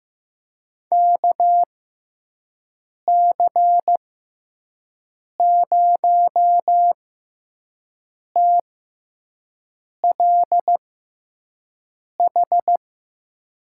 Morse code, heard as a keyed tone, about 15 wpm, spaced as for 5 wpm, 700 Hz. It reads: KC0TLH
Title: USA Call Signs - 6 Characters - 6x Character Spacing - 15wpm